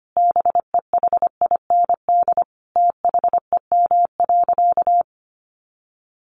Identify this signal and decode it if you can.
Morse code, heard as a keyed tone, about 25 wpm, 700 Hz. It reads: BEHIND THEM.
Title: Sentences from Top 1500 Words 25wpm